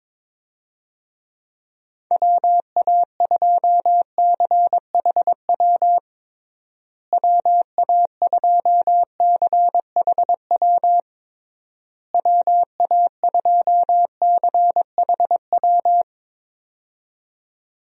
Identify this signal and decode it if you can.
Morse code, heard as a keyed tone, about 22 wpm, 700 Hz. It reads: WA2CHW WA2CHW WA2CHW
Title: USA Call Signs - 6 Characters - Repeated 3x - 22wpm